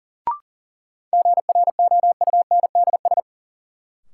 Morse code, heard as a keyed tone, about 40 wpm, 700 Hz. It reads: GROUNDS
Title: Sentences from Top 1300 Words - New Word Drill - 40wpm